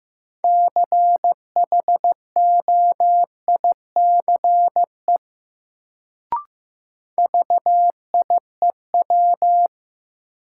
Morse code, heard as a keyed tone, about 15 wpm, 700 Hz. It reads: CHOICE VIEW
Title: Sets of 5 Words - New Word Drill - 15wpm